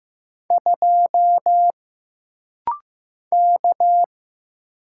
Morse code, heard as a keyed tone, about 15 wpm, 700 Hz. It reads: 2 K
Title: Single Letter-Number Alt - Rapid-Fire 15wpm